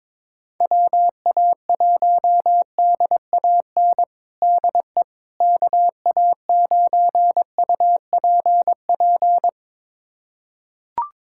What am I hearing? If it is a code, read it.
Morse code, heard as a keyed tone, about 22 wpm, 700 Hz. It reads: WA1DAN DE KA9UPP
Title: USA Callsign DE Callsign 22wpm